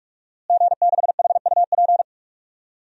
Morse code, heard as a keyed tone, about 43 wpm, 700 Hz. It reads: G6HUP
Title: DXCC-100 Call Signs - 5 Characters - Repeated 2x - 45wpm